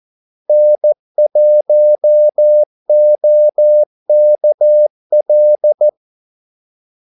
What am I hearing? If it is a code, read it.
Morse code, heard as a keyed tone, about 14 wpm, 600 Hz. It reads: N1OKL
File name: Callsigns-14wpm